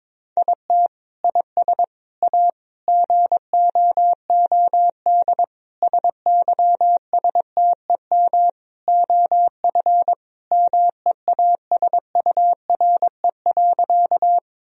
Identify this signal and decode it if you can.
Morse code, heard as a keyed tone, about 21 wpm, 700 Hz. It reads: IT IS A GOOD SYSTEM OF MEASURE.
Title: Sentences from Top 1000 Words - Review Part 3 - 22wpm